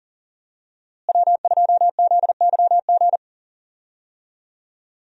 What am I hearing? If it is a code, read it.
Morse code, heard as a keyed tone, about 37 wpm, 700 Hz. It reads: W2ZYG
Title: USA Call Signs - 5 Characters - Repeated 2x - 40wpm